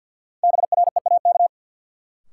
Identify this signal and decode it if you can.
Morse code, heard as a keyed tone, about 50 wpm, 700 Hz. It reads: BREAK
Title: Sentences from Top 900 Words - New Word Drill - 50wpm